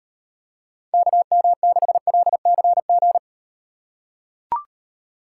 Morse code, heard as a keyed tone, about 37 wpm, 700 Hz. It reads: KM6LCG